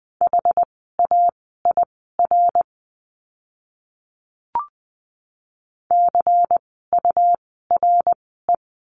Morse code, heard as a keyed tone, about 20 wpm, spaced as for 14 wpm, 700 Hz. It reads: HAIR CURE
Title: 101 Lesson 17f - Letter u - Words 20wpm